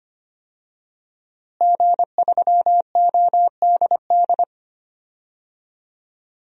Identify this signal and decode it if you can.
Morse code, heard as a keyed tone, about 25 wpm, 700 Hz. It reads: G3ODD
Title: DXCC-100 Call Signs - 5 Characters - Repeated 3x - 25wpm